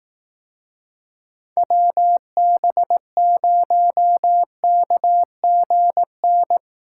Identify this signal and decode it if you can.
Morse code, heard as a keyed tone, about 18 wpm, 700 Hz. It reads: WB0KGN
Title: Speed Racing - USA Call Signs 15wpm